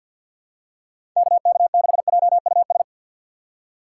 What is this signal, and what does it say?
Morse code, heard as a keyed tone, about 50 wpm, 700 Hz. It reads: KK6JUS
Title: USA Call Signs - 3x Character Spacing - 50wpm